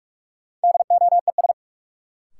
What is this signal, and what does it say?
Morse code, heard as a keyed tone, about 44 wpm, 700 Hz. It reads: DOES
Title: Sets of 2 Words - New Word Drill - 45wpm